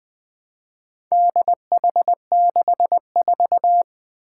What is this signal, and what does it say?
Morse code, heard as a keyed tone, about 20 wpm, 700 Hz. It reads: DH64
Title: Four Challenging Letter-Number Combinations 20wpm